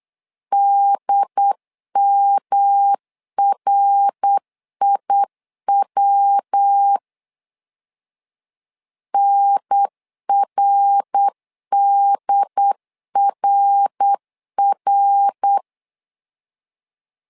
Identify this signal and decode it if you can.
Morse code, heard as a keyed tone, about 8 wpm, 800 Hz. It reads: DMRIW NRDRR
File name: LEZIONI / 3 MWRO / 3 MWR 8wpm 2444